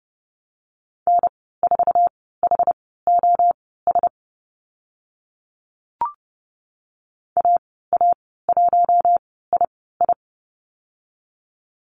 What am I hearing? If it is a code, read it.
Morse code, heard as a keyed tone, about 30 wpm, spaced as for 17 wpm, 690 Hz. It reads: N4HOS AA1II
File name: Lesson 15g - Number 5 - Call Signs 30wpm